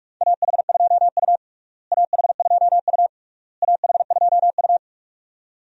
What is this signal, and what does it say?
Morse code, heard as a keyed tone, about 41 wpm, 700 Hz. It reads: AH2U AH2U AH2U
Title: USA Call Signs - 4 Characters - Repeated 3x - 45wpm